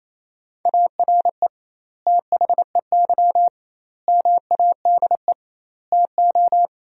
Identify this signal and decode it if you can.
Morse code, heard as a keyed tone, about 28 wpm, 700 Hz. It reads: ARE THEY MADE TO
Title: 2x Word-Spacing - Sentences from Top 400 Words - 28wpm